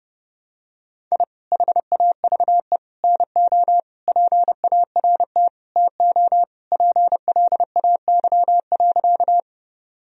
Morse code, heard as a keyed tone, about 30 wpm, 700 Hz. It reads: I HAVE NO PART TO PLAY.
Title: Sentences from Top 1000 Words - Review Part 1 - 30wpm